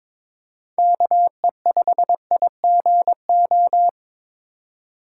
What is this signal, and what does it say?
Morse code, heard as a keyed tone, about 22 wpm, 700 Hz. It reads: KE5IGO
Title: USA Call Signs - 6 Characters - 4x Character Spacing - 22wpm